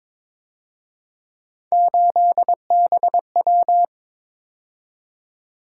Morse code, heard as a keyed tone, about 26 wpm, 700 Hz. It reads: 8BW
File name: Speed-Racing - Three Letter-Number Combinations - 22wpm